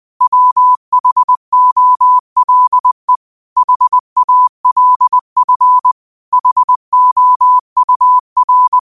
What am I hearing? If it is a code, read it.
Morse code, heard as a keyed tone, about 20 wpm, 1000 Hz. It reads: WHOLE HALF HOUR